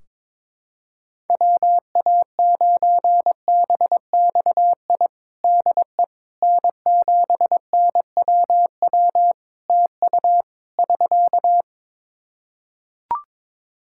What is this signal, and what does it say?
Morse code, heard as a keyed tone, about 22 wpm, 700 Hz. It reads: WA9BXI DE N7NWW TU <SK>
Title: Practice QSO - Exchange 4 - Ending - 22wpm